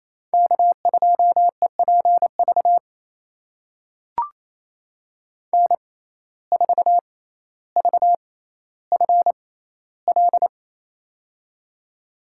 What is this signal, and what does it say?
Morse code, heard as a keyed tone, about 27 wpm, spaced as for 14 wpm, 700 Hz. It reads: K2EPV N4VFL
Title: USA Call Signs - 5 Characters - 6x Character Spacing - 28wpm